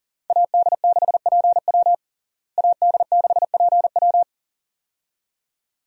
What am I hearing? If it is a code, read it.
Morse code, heard as a keyed tone, about 40 wpm, 700 Hz. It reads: AD6PW AD6PW
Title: USA Call Signs - 5 Characters - Repeated 2x - 40wpm